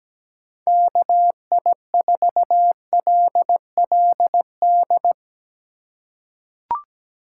Morse code, heard as a keyed tone, about 17 wpm, 700 Hz. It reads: KI4LLD
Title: USA Call Signs - 4x Character Spacing - 17wpm